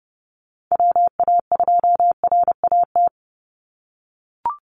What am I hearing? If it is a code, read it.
Morse code, heard as a keyed tone, about 30 wpm, 700 Hz. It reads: WA2RAT
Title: USA Call Signs 30wpm